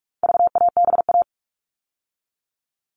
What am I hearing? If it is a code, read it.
Morse code, heard as a keyed tone, about 44 wpm, 700 Hz. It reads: VABA
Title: All Cut Numbers - 4-Digit 45wpm